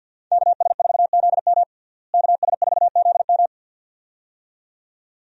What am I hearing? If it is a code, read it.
Morse code, heard as a keyed tone, about 48 wpm, 700 Hz. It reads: KS4ZM KS4ZM
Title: USA Call Signs - 5 Characters - Repeated 2x - 50wpm